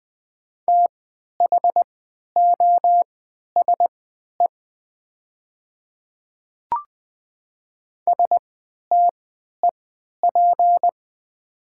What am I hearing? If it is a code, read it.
Morse code, heard as a keyed tone, about 20 wpm, spaced as for 11 wpm, 700 Hz. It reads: THOSE STEP
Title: 3x Character Spacing - Top 500 Words - 20wpm